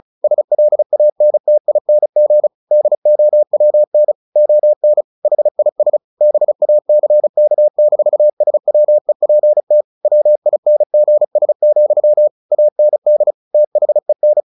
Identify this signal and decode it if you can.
Morse code, heard as a keyed tone, about 35 wpm, 600 Hz. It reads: SLANTING DOWN ON HIS BACK-SWEPT WINGS, AND THEN
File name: the old man and the sea / the old man and the sea 1 35wpm